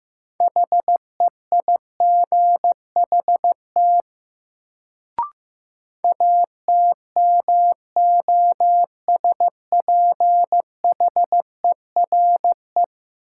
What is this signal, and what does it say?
Morse code, heard as a keyed tone, about 15 wpm, 700 Hz. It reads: HEIGHT ATMOSPHERE
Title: Sentences from Top 1600 Words - New Word Drill - 15wpm